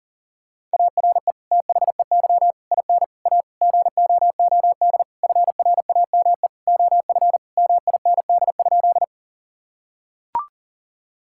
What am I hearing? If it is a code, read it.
Morse code, heard as a keyed tone, about 40 wpm, 700 Hz. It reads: ARE THEY IN A GOOD FRAME OF MIND?
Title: Sentences from Top 1000 Words - Review Part 6 - 40wpm